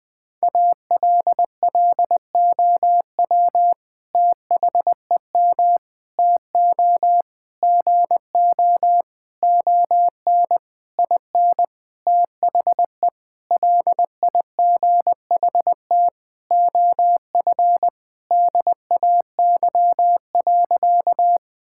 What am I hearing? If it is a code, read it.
Morse code, heard as a keyed tone, about 20 wpm, 700 Hz. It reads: ALLOW THEM TO GO ON IN THE LIGHT OF DAY.
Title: Sentences from Top 1000 Words - Review Part 5 - 20wpm